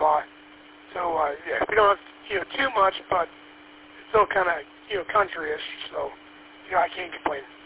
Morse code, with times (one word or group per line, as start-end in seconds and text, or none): none